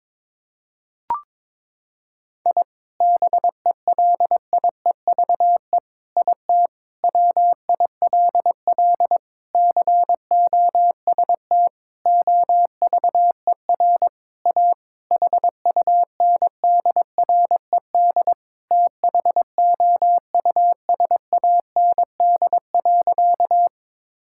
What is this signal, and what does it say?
Morse code, heard as a keyed tone, about 22 wpm, 700 Hz. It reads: I BELIEVE IT WILL COST OVER A HUNDRED THOUSAND.
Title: Sentences from Top 1000 Words - Review Part 5 - 22wpm